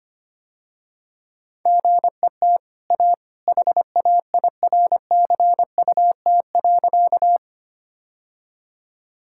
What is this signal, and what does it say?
Morse code, heard as a keyed tone, about 25 wpm, 700 Hz. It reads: GET A HAIRCUT.
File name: Easy Sentences with Hints - 25wpm